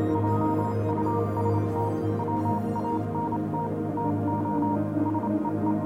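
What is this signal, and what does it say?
Morse code, heard as a keyed tone, about 27 wpm, 940 Hz, with noise. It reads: JUST WANT TO SEE
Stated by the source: Morse code version of a music track.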